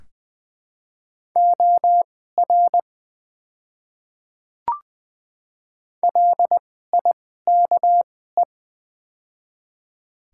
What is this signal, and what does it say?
Morse code, heard as a keyed tone, about 20 wpm, spaced as for 15 wpm, 700 Hz. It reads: OR LIKE
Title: 2x Character Spacing - Top 200 Words - 20wpm